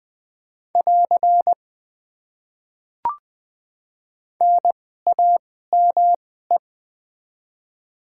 Morse code, heard as a keyed tone, about 20 wpm, spaced as for 14 wpm, 700 Hz. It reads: <AR> NAME
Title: Top 90 QSO Elements - 2x Character Spacing - 20wpm